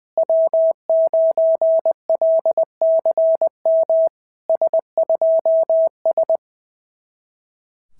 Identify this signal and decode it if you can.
Morse code, heard as a keyed tone, about 20 wpm, 660 Hz. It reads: W9LCM S2S
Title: SOTA QSO Practice 20wpm